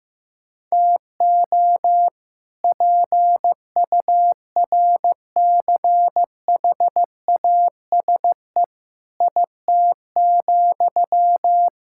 Morse code, heard as a keyed tone, about 16 wpm, 700 Hz. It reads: TO PURCHASE IT,
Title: Sentences from Top 1000 Words - Review Part 5 - 15wpm